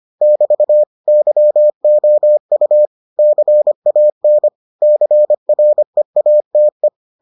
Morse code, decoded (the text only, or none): <BT> YOU CAN CREATE